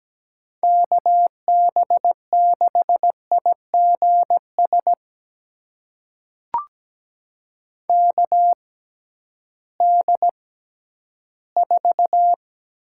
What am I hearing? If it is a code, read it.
Morse code, heard as a keyed tone, about 17 wpm, spaced as for 11 wpm, 700 Hz. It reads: KB6IGS KD4
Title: USA Call Signs - 6 Characters - 6x Character Spacing - 17wpm